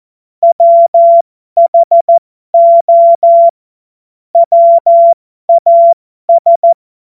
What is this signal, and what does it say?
Morse code, heard as a keyed tone, about 14 wpm, spaced as for 12 wpm, 680 Hz. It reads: WHO WAS
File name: Morsekurs / Witze